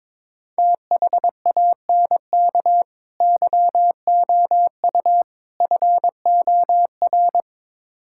Morse code, heard as a keyed tone, about 23 wpm, 700 Hz. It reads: THANK YOU FOR
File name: practice / 22wpm / Build A Phrase - 5-Word Phrases - 22wpm